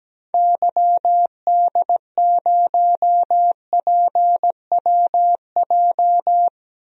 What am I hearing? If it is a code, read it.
Morse code, heard as a keyed tone, about 17 wpm, 700 Hz. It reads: YD0PWJ
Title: DXCC-100 Call Signs - 6 Characters - 17wpm